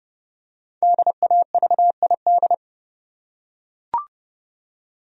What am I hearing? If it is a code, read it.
Morse code, heard as a keyed tone, about 30 wpm, 700 Hz. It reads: DAVID
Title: USA Common Names - Max 5 Letters - 30wpm